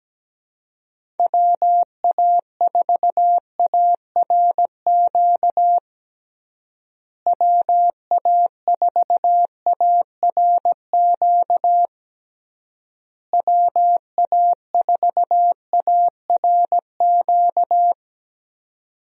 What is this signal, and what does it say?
Morse code, heard as a keyed tone, about 17 wpm, 700 Hz. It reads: WA4ARQ WA4ARQ WA4ARQ
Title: USA Call Signs - 6 Characters - Repeated 3x - 17wpm